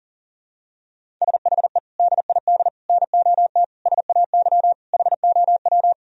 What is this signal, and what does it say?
Morse code, heard as a keyed tone, about 40 wpm, 700 Hz. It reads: SHE DID NOT SAY HOW
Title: Sentences from Top 1000 Words - Review Part 1 - 40wpm